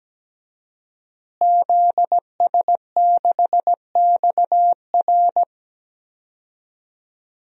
Morse code, heard as a keyed tone, about 17 wpm, 700 Hz. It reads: ZS6XR
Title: DXCC-100 Call Signs - 5 Characters - Repeated 3x - 17wpm